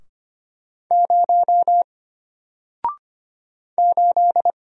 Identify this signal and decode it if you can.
Morse code, heard as a keyed tone, about 25 wpm, 700 Hz. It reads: 0 8
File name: Single-Letter-Number-25wpm